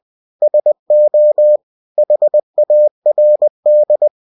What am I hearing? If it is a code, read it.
Morse code, heard as a keyed tone, about 20 wpm, 600 Hz. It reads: SO HARD